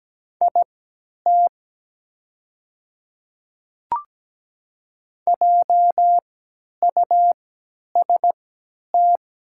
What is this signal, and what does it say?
Morse code, heard as a keyed tone, about 17 wpm, spaced as for 10 wpm, 700 Hz. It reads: IT JUST
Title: 3x Character Spacing - Top 500 Words - 17wpm